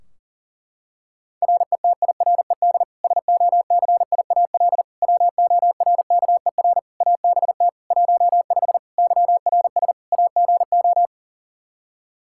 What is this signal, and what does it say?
Morse code, heard as a keyed tone, about 40 wpm, 700 Hz. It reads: RETIRED SOCIAL WORKER ABT 15 YRS AGO
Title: K1USN SST QSO Practice 40wpm